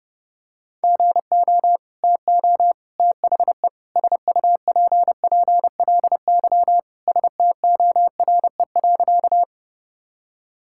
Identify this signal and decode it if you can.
Morse code, heard as a keyed tone, about 31 wpm, 700 Hz. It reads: GO TO THE SUPPLY STORE.
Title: Sentences from Top 1000 Words - Review Part 5 - 30wpm